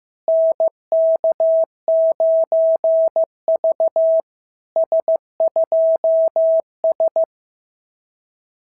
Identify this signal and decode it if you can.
Morse code, heard as a keyed tone, about 14 wpm, 650 Hz. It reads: NK9V S2S
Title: SOTA QSO Practice 15wpm